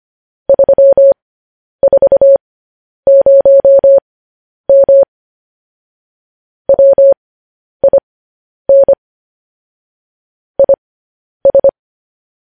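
Morse code, heard as a keyed tone, about 25 wpm, spaced as for 10 wpm, 550 Hz. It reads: 340M WIN IS
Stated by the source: Morse code news headlines.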